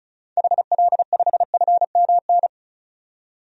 Morse code, heard as a keyed tone, about 36 wpm, 700 Hz. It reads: HL5FMN
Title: DXCC-100 Call Signs - 6 Characters - 35wpm